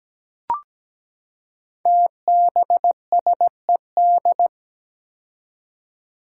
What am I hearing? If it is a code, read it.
Morse code, heard as a keyed tone, about 17 wpm, 700 Hz. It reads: TBSED
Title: Five Challenging Letter-Number Combinations 17wpm